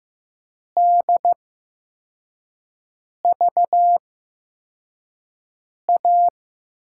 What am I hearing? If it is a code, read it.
Morse code, heard as a keyed tone, about 15 wpm, spaced as for 4 wpm, 700 Hz. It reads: DVA
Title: USA Call Signs - 6 Characters - 8x Character Spacing - 15wpm